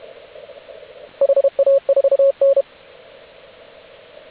Morse code, heard as a keyed tone, about 32 wpm, 550 Hz. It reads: HA4N
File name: AUDIO2019 / HG8DX CW